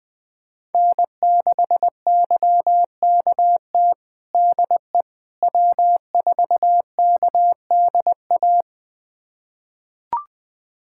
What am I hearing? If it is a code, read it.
Morse code, heard as a keyed tone, about 20 wpm, 700 Hz. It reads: N6YKT DE W4KDA
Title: USA Callsign DE Callsign - 5 Characters - 20wpm